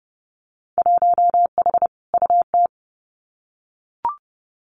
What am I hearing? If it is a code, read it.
Morse code, heard as a keyed tone, about 30 wpm, 700 Hz. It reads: 1H UT
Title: Winter Field Day Exchanges Encore 30wpm